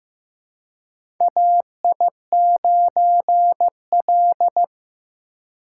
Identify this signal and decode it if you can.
Morse code, heard as a keyed tone, about 15 wpm, 700 Hz. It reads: AI9L
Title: USA Call Signs - 4 Characters - 3x Character Spacing - 15wpm